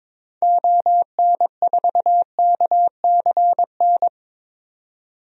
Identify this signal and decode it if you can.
Morse code, heard as a keyed tone, about 22 wpm, 700 Hz. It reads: ON4KCN
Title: DXCC-100 Call Signs - 6 Characters - 22wpm